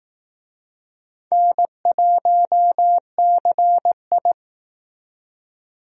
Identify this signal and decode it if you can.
Morse code, heard as a keyed tone, about 18 wpm, 700 Hz. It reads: N1CI
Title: Speed Racing - USA 1x2 Call Signs 15wpm